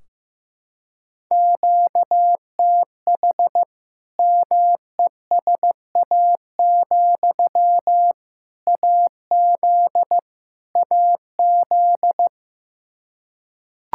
Morse code, heard as a keyed tone, about 15 wpm, 700 Hz. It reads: QTH MESA, AZ AZ
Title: Practice QSO - Exchange 1 - Beginning - 15wpm